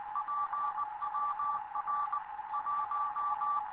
Morse code, heard as a keyed tone, about 19 wpm, 1160 Hz, with radio noise. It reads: PWR 1
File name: Morze-HA2MN / hangp / 2 11 GYAKORLAT